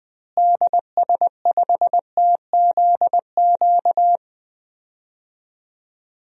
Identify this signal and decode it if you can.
Morse code, heard as a keyed tone, about 20 wpm, 700 Hz. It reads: DS5TZQ
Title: DXCC-100 Call Signs - 6 Characters - Repeated 3x - 20wpm